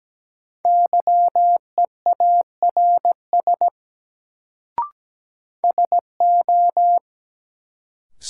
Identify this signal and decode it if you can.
Morse code, heard as a keyed tone, about 17 wpm, 700 Hz. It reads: YEARS SO
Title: Sets of 2 Words - New Word Drill - 17wpm